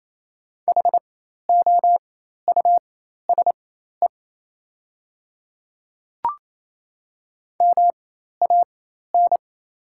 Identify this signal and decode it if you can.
Morse code, heard as a keyed tone, about 28 wpm, spaced as for 13 wpm, 700 Hz. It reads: HOUSE MAN
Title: 4x Character Spacing - Top 300 Words - 28wpm